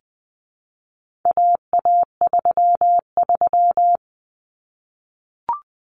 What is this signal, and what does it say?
Morse code, heard as a keyed tone, about 20 wpm, 700 Hz. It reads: AA33
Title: Common Cut Numbers - 4-Digit 20wpm